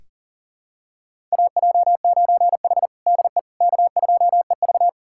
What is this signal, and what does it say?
Morse code, heard as a keyed tone, about 40 wpm, 700 Hz. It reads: AJ9H DE K2EV
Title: USA Callsign DE Callsign - 4 Characters - 40wpm